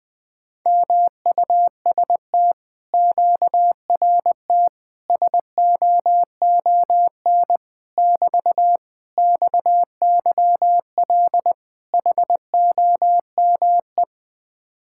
Morse code, heard as a keyed tone, about 20 wpm, 700 Hz. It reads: MUST QRT SOON <BT> XYL HOME